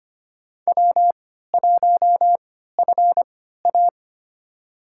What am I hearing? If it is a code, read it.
Morse code, heard as a keyed tone, about 25 wpm, spaced as for 14 wpm, 700 Hz. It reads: W1FA
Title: USA Call Signs - 4 Characters - 3x Character Spacing - 25wpm